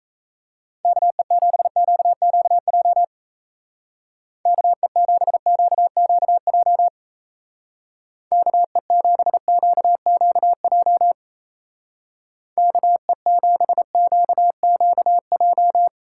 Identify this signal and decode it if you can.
Morse code, heard as a keyed tone, about 34 wpm, 700 Hz. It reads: KE7QQJ KE7QQJ KE7QQJ KE7QQJ
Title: Speed Racing - USA 2x3 Call Signs 28wpm